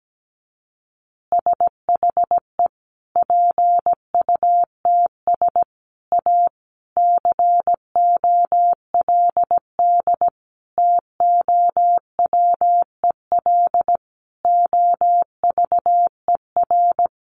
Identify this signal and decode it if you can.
Morse code, heard as a keyed tone, about 17 wpm, 700 Hz. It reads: SHE PUTS A COLD TOWEL OVER
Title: Easy Phrases 17wpm